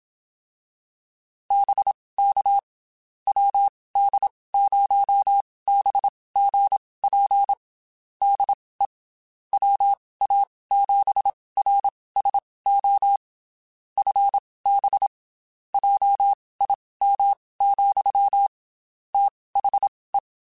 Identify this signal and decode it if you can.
Morse code, heard as a keyed tone, about 27 wpm, spaced as for 19 wpm, 800 Hz. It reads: BK WD0BGP DE WA7RSO FB JIM, THE
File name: QSOs / QSOs-mp3s / WD0FBT 1@25